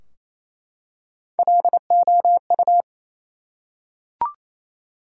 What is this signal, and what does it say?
Morse code, heard as a keyed tone, about 28 wpm, 700 Hz. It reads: LOU
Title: USA Common Names - Max 3 Letters - 28wpm